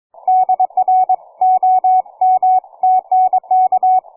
Morse code, heard as a keyed tone, about 23 wpm, 740 Hz. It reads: DR OM TNK
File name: cw forum / 9m2ax 160m1829 22h14